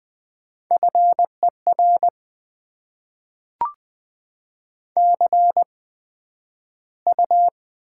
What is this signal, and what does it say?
Morse code, heard as a keyed tone, about 20 wpm, spaced as for 11 wpm, 700 Hz. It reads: FER CU
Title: Top 20 QSO Elements - 8x Character Spacing - 20wpm